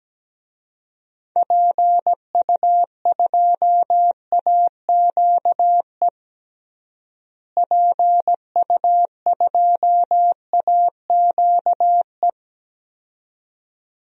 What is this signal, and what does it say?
Morse code, heard as a keyed tone, about 17 wpm, 700 Hz. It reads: PU2AQE PU2AQE
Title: DXCC-100 Call Signs - 6 Characters - Repeated 2x - 17wpm